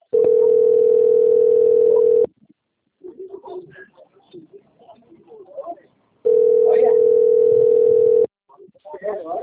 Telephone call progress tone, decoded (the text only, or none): ringback tone